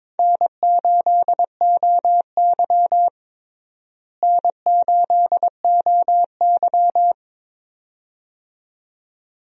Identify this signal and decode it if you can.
Morse code, heard as a keyed tone, about 22 wpm, 700 Hz. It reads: N8OY N8OY
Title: USA Call Signs - 4 Characters - Repeated 2x - 22wpm